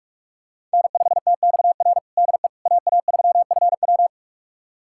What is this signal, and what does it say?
Morse code, heard as a keyed tone, about 46 wpm, 700 Hz. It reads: N5TXR DE AA3FW
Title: USA Callsign DE Callsign - 5 Characters - 45wpm